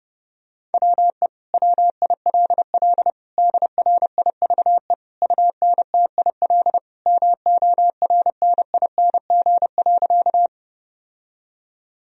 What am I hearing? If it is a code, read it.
Morse code, heard as a keyed tone, about 32 wpm, 700 Hz. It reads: WE WILL DRIVE UNTIL MORNING.
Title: Sentences from Top 1000 Words - Review Part 2 - 30wpm